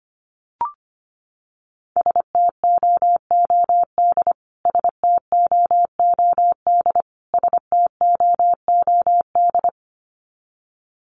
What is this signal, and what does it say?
Morse code, heard as a keyed tone, about 25 wpm, 700 Hz. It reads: STOOD STOOD STOOD
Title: Top 500 Words Repeated 25wpm